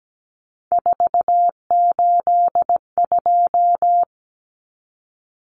Morse code, heard as a keyed tone, about 17 wpm, 700 Hz. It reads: 482
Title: Common Cut Numbers - 3-Digit 17wpm